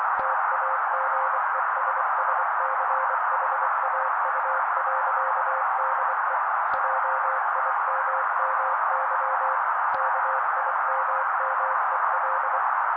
Morse code, heard as a keyed tone, about 24 wpm, 530 Hz. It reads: TAGESSCHAU.DE JIMMY KIMMEL
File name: daily / 23wpm noise